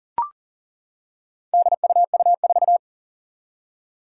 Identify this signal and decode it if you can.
Morse code, heard as a keyed tone, about 40 wpm, 700 Hz. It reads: DUU4
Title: Four Challenging Letter-Number Combinations 40wpm